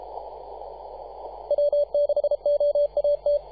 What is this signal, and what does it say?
Morse code, heard as a keyed tone, about 33 wpm, 580 Hz, with radio noise. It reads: W6OAT
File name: posts / 2015-sept-24-2118z-t2gc-18-071-cw